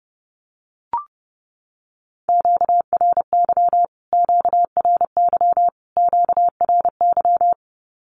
Morse code, heard as a keyed tone, about 30 wpm, 700 Hz. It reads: QRY QRY QRY
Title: Q-Codes Repeated 3x 30wpm